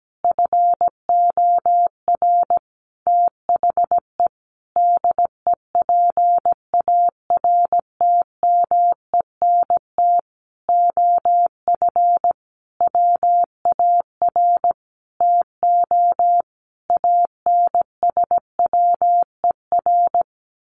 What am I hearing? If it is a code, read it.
Morse code, heard as a keyed tone, about 17 wpm, 700 Hz. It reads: FOR THE DEPARTMENT OF WAR TO ANSWER
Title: Sentences from Top 1500 Words 17wpm